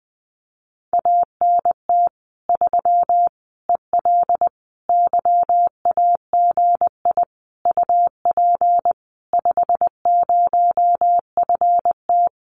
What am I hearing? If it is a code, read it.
Morse code, heard as a keyed tone, about 20 wpm, 700 Hz. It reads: ANT 3 EL YAGI UP 50FT
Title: Common QSO Phrases Encore 20wpm